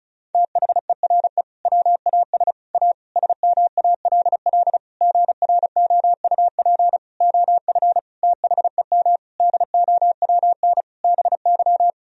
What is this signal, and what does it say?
Morse code, heard as a keyed tone, about 34 wpm, 700 Hz. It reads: THERE WAS A SMALL GROUP OF THEM DOWN BY
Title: Sentences from Top 1000 Words - Review Part 1 - 35wpm